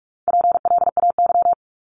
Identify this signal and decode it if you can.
Morse code, heard as a keyed tone, about 46 wpm, 700 Hz. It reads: PLAY
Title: Top 200 Words 45wpm